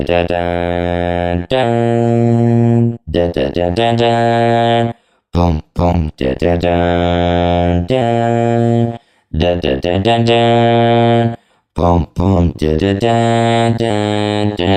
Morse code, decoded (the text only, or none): none